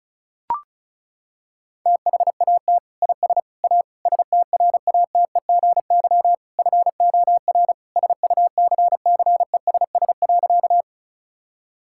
Morse code, heard as a keyed tone, about 34 wpm, 700 Hz. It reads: THAT IS A STRATEGY FOR SUCCESS.
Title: Sentences from Top 1000 Words - Review Part 5 - 35wpm